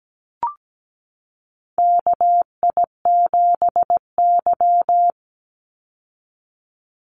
Morse code, heard as a keyed tone, about 17 wpm, 700 Hz. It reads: KI7Y
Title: Call Signs Repeated 2x - 17wpm